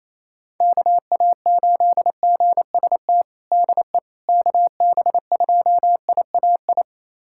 Morse code, heard as a keyed tone, about 28 wpm, 700 Hz. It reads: KA8GST DE KB2IAI
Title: USA Callsign DE Callsign - 6 Characters - 28wpm